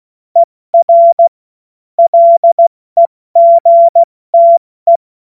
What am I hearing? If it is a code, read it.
Morse code, heard as a keyed tone, about 16 wpm, 680 Hz. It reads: ER LEGTE
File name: Morsekurs / Witze / Witz 10. Mai 2022 70 F80